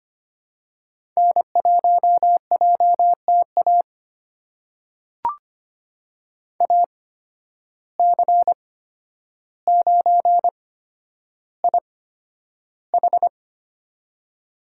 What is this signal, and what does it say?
Morse code, heard as a keyed tone, about 25 wpm, spaced as for 11 wpm, 700 Hz. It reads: N1JTA AC9IH
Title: USA Call Signs - 5 Characters - 8x Character Spacing - 25wpm